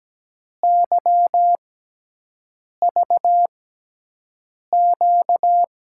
Morse code, heard as a keyed tone, about 17 wpm, spaced as for 6 wpm, 700 Hz. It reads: YVQ
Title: USA Call Signs - 6 Characters - 6x Character Spacing - 17wpm